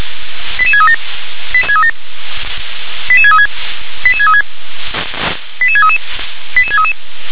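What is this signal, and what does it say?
Five-tone selective call (ZVEI), heard as quick runs of five tones, tones of about 70 ms. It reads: 88637 88636 88636 88633 88633
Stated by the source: ZVEI five-tone selective call received on SDR.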